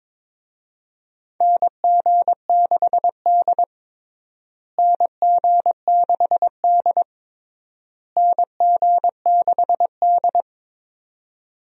Morse code, heard as a keyed tone, about 22 wpm, 700 Hz. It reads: NG6D NG6D NG6D
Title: USA Call Signs - 4 Characters - Repeated 3x - 22wpm